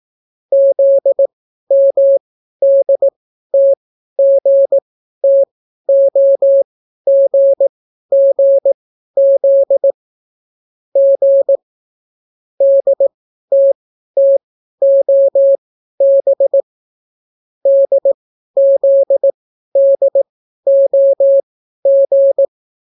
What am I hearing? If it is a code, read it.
Morse code, heard as a keyed tone, about 18 wpm, spaced as for 12 wpm, 550 Hz. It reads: ZMDTGTOGGZ G DTTOB DZDOG